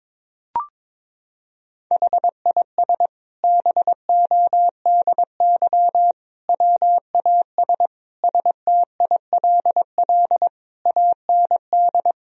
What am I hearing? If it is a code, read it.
Morse code, heard as a keyed tone, about 22 wpm, 700 Hz. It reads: HIS BODY WAS STILL AND
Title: Sentences from Top 1000 Words - Review Part 2 - 22wpm